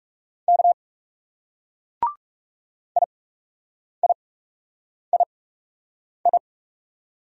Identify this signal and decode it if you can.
Morse code, heard as a keyed tone, about 38 wpm, 700 Hz. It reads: K I I I I